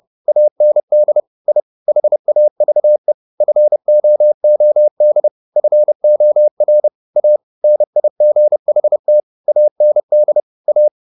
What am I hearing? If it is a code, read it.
Morse code, heard as a keyed tone, about 30 wpm, 600 Hz. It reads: AND I HAVE FOOD FOR A NIGHT AND A